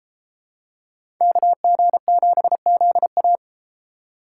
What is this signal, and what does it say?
Morse code, heard as a keyed tone, about 32 wpm, 700 Hz. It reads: KG7ZA